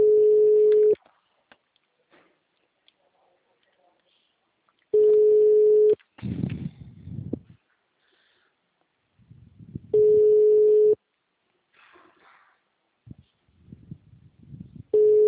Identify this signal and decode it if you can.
Ringback tone (Europe, 425 Hz), with speech.
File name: out- -2023- - - .28